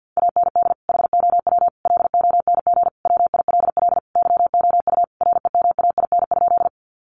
Morse code, heard as a keyed tone, about 48 wpm, 700 Hz. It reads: AND HOW LONG WILL YOU REMAIN?
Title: Sentences from Top 1200 Words 50wpm